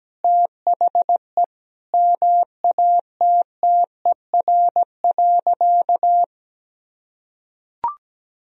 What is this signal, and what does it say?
Morse code, heard as a keyed tone, about 17 wpm, 700 Hz. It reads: THE MATTER.
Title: Sentences from Top 1000 Words - Review Part 4 - 17wpm